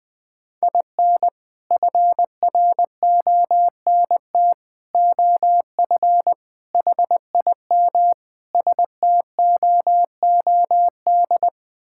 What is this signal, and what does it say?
Morse code, heard as a keyed tone, about 20 wpm, 700 Hz. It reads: IN FRONT OF HIM STOOD
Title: Sentences with 4-Word Phrases - 20wpm